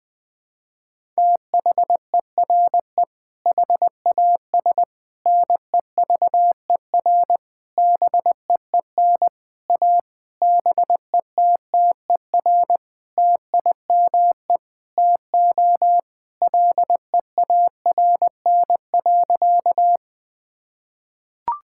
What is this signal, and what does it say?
Morse code, heard as a keyed tone, about 20 wpm, 700 Hz. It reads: THERE HAS NEVER BEEN A BETTER TIME TO LEARN.
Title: Sentences with 8-Word Phrases - Spoken Beginning - 20wpm